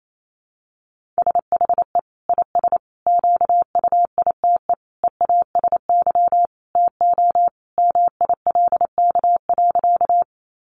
Morse code, heard as a keyed tone, about 28 wpm, 700 Hz. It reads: SHE IS QUITE EASY TO MILK.
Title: Sentences from Top 1500 Words 28wpm